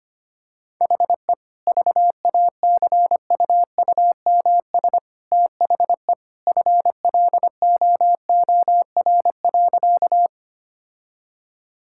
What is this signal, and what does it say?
Morse code, heard as a keyed tone, about 25 wpm, 690 Hz. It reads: HE VACUUMS THE FLOOR.
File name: Easy Sentences with Hints - 25wpm